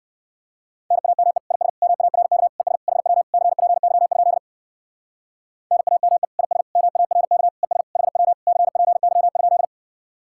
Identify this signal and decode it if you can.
Morse code, heard as a keyed tone, about 58 wpm, 700 Hz. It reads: NAME IS MARK ES HW CPY? NAME IS MARK ES HW CPY?